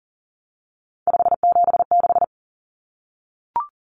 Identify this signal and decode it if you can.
Morse code, heard as a keyed tone, about 40 wpm, 700 Hz. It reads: 576
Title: Numbers - 3-Digit 40wpm